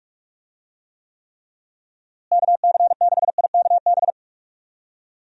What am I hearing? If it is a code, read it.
Morse code, heard as a keyed tone, about 46 wpm, 700 Hz. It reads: KC6IKB
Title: USA Call Signs - 6 Characters - Repeated 3x - 45wpm